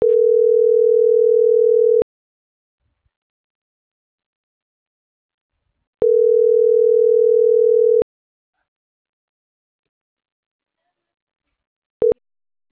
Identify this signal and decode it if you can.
Ringback tone (North America), 440 + 480 Hz.